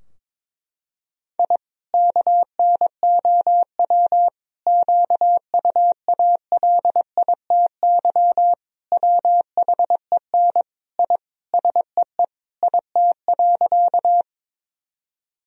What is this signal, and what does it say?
Morse code, heard as a keyed tone, about 22 wpm, 700 Hz. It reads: I KNOW QUALITY WHEN I SEE IT.
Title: Sentences from Top 1000 Words - Review Part 4 - 22wpm